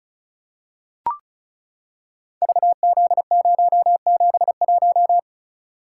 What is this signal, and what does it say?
Morse code, heard as a keyed tone, about 36 wpm, 700 Hz. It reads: VZ071
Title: Code Groups - 5 Letter:Numbers - 35wpm